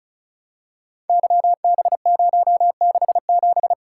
Morse code, heard as a keyed tone, about 36 wpm, 700 Hz. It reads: YB067
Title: Code Groups - 5 Letter:Numbers - 35wpm